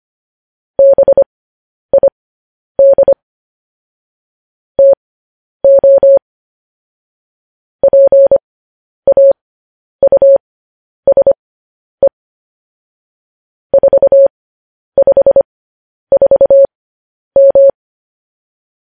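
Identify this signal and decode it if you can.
Morse code, heard as a keyed tone, about 25 wpm, spaced as for 10 wpm, 550 Hz. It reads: BID TO PAUSE 454M
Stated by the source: Morse code news headlines sent as a reading transcription, preceded by vvv.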